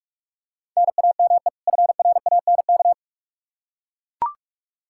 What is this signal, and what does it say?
Morse code, heard as a keyed tone, about 42 wpm, 700 Hz. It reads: NAME FRANK
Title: K1USN SST QSO Practice 45wpm